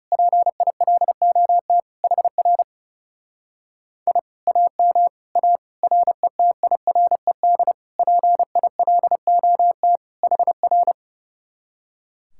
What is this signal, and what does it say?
Morse code, heard as a keyed tone, about 28 wpm, 700 Hz. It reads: PILOT HR I AM A RETIRED PILOT HR
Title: Speed-Racing Common QSO Phrases - 30wpm